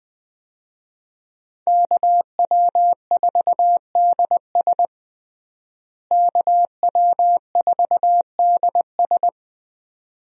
Morse code, heard as a keyed tone, about 20 wpm, 700 Hz. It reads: KW4DS KW4DS
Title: USA Call Signs - 5 Characters - Repeated 2x - 20wpm